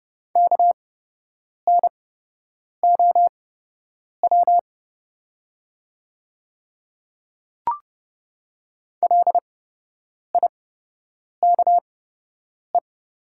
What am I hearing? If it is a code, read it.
Morse code, heard as a keyed tone, about 30 wpm, spaced as for 8 wpm, 700 Hz. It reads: KNOW LIKE
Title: 8x Character Spacing - Top 100 Words - 30wpm